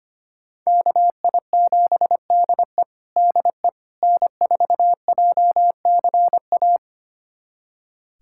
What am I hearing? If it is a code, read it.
Morse code, heard as a keyed tone, about 25 wpm, 700 Hz. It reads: KI7DE DE N4JCA
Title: USA Callsign DE Callsign - 5 Characters - 25wpm